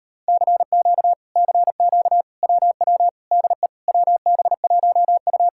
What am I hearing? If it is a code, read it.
Morse code, heard as a keyed tone, about 38 wpm, 700 Hz. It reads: CQ CQ WW DE WB1U